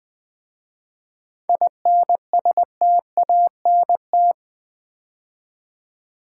Morse code, heard as a keyed tone, about 20 wpm, 700 Hz. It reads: INSTANT
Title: 7 Letter Words - First 3 Letters Given - 20wpm